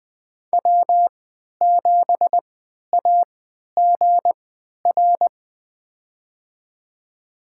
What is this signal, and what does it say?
Morse code, heard as a keyed tone, about 20 wpm, spaced as for 11 wpm, 700 Hz. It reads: W7AGR
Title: USA Call Signs - 3x Character Spacing - 20wpm